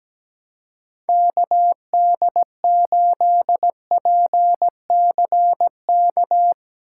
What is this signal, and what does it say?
Morse code, heard as a keyed tone, about 18 wpm, 700 Hz. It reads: KD8PCK
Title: Speed Racing - USA 2x3 Call Signs 17wpm